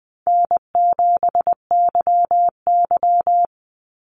Morse code, heard as a keyed tone, about 20 wpm, 700 Hz. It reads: N7YY
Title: NA SOTA Chaser Callsigns 20wpm